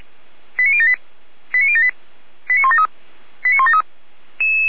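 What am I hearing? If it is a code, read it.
Five-tone selective call (ZVEI), heard as quick runs of five tones, tones of about 80 ms. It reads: 89078 89078 89173 89173
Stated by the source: ZVEI five-tone selective call received by SDR.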